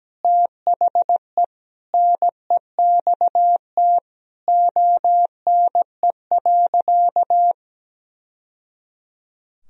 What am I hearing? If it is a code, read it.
Morse code, heard as a keyed tone, about 17 wpm, 700 Hz. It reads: THE NEXT ONE.
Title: Sentences from Top 1000 Words - Review Part 1 - 17wpm